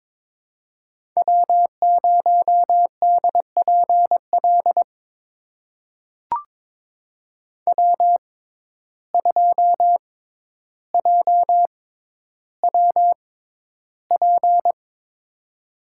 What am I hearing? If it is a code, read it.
Morse code, heard as a keyed tone, about 22 wpm, spaced as for 11 wpm, 700 Hz. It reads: W0DPL W2JWP
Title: USA Call Signs - 5 Characters - 6x Character Spacing - 22wpm